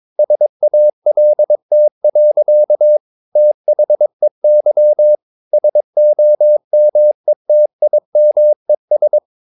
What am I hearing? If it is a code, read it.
Morse code, heard as a keyed tone, about 23 wpm, 600 Hz. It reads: SALT. THEY SOMETIMES
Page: Romanen Never af Ken Follet på engelsk.